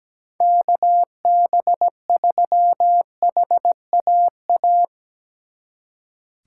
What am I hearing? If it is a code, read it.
Morse code, heard as a keyed tone, about 19 wpm, 700 Hz. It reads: KB3HAA
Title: Speed Racing - USA 2x3 Call Signs 17wpm